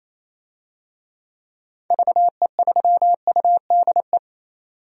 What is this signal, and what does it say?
Morse code, heard as a keyed tone, about 28 wpm, 700 Hz. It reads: VE3UDE
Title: DXCC-100 Call Signs - 6 Characters - 28wpm